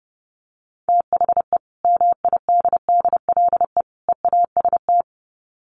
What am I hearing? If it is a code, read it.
Morse code, heard as a keyed tone, about 30 wpm, 700 Hz. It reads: THE MIDDLE EAST
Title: Sets of 3 Words 30wpm